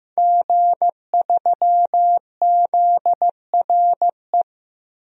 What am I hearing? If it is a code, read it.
Morse code, heard as a keyed tone, about 15 wpm, 700 Hz. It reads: G3ZRE
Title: DXCC-100 Call Signs - 5 Characters - 15wpm